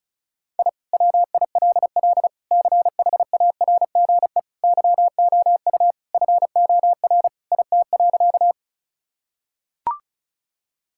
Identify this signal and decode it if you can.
Morse code, heard as a keyed tone, about 34 wpm, 700 Hz. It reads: I WILL CHARGE YOU FOR IT.
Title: Sentences from Top 1000 Words - Review Part 6 - 35wpm